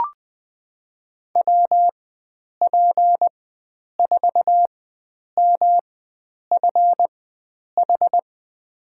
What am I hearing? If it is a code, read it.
Morse code, heard as a keyed tone, about 20 wpm, spaced as for 9 wpm, 700 Hz. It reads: WP4MFH
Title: USA Call Signs - 6 Characters - 4x Character Spacing - 20wpm